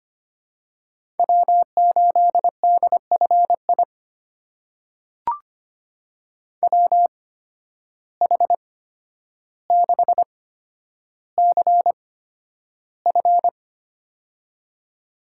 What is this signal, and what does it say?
Morse code, heard as a keyed tone, about 25 wpm, spaced as for 11 wpm, 700 Hz. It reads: W8DFI WH6CF
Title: USA Call Signs - 8x Character Spacing - 25wpm